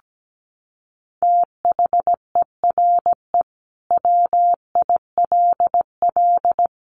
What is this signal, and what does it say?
Morse code, heard as a keyed tone, about 17 wpm, 700 Hz. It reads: THERE WILL
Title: Sets of 4 Words 17wpm